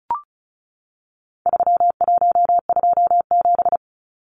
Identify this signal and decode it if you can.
Morse code, heard as a keyed tone, about 34 wpm, 700 Hz. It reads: 3127
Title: Numbers - 4-Digit 35wpm